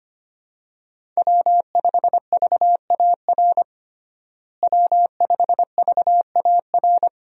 Morse code, heard as a keyed tone, about 25 wpm, 700 Hz. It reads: W5VAR W5VAR
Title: USA Call Signs - 5 Characters - Repeated 3x - 25wpm